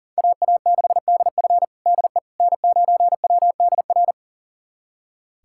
Morse code, heard as a keyed tone, about 37 wpm, 700 Hz. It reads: AA6DF DE N9WDR
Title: USA Callsign DE Callsign - 5 Characters - 40wpm